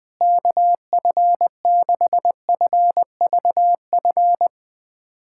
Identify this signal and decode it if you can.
Morse code, heard as a keyed tone, about 21 wpm, 700 Hz. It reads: KF6FVF
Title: USA Callsign DE Callsign - 6 Characters - 20wpm